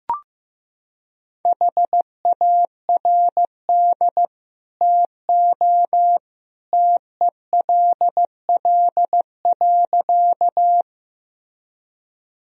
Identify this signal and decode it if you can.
Morse code, heard as a keyed tone, about 16 wpm, 700 Hz. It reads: HARD TO TELL.
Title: Sentences from Top 1000 Words - Review Part 1 - 15wpm